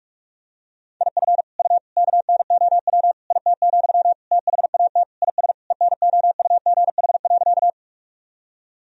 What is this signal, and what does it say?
Morse code, heard as a keyed tone, about 46 wpm, 700 Hz. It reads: IF U KNOW IT, THAT IS ENOUGH.
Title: Sentences from Top 400 Words and Abbreviations 45wpm